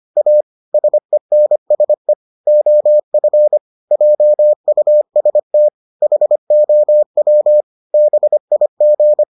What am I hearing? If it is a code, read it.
Morse code, heard as a keyed tone, about 25 wpm, 600 Hz. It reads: A SENSE OF JUST HOW BIG